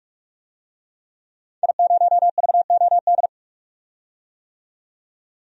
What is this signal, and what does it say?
Morse code, heard as a keyed tone, about 43 wpm, 700 Hz. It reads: I0VOD
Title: DXCC-100 Call Signs - 5 Characters - 45wpm